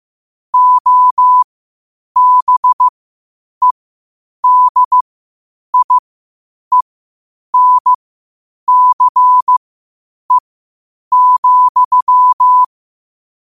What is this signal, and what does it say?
Morse code, heard as a keyed tone, about 15 wpm, spaced as for 8 wpm, 1000 Hz. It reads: OBEDIENCE,
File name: qotdfiles / 8wpm